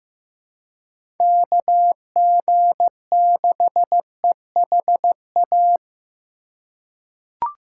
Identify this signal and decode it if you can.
Morse code, heard as a keyed tone, about 16 wpm, 700 Hz. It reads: KG6EHA
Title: USA Call Signs - 6 Characters - 2x Character Spacing - 15wpm